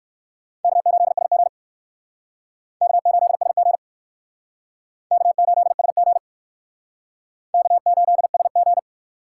Morse code, heard as a keyed tone, about 60 wpm, 700 Hz. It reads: K8SG K8SG K8SG K8SG